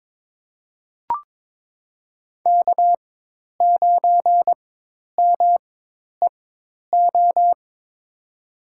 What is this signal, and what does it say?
Morse code, heard as a keyed tone, about 21 wpm, spaced as for 10 wpm, 700 Hz. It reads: K9MEO
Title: USA Call Signs - 4x Character Spacing - 22wpm